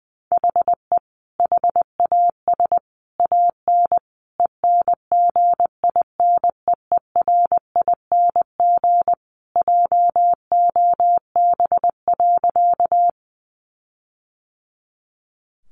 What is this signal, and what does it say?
Morse code, heard as a keyed tone, about 20 wpm, 700 Hz. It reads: HE HAS AN ENGINEERING JOB.
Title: Sentences from Top 1500 Words 20wpm